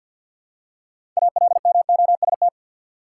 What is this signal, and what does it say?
Morse code, heard as a keyed tone, about 52 wpm, 700 Hz. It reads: ALMOST
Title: Six Letter Words - 50wpm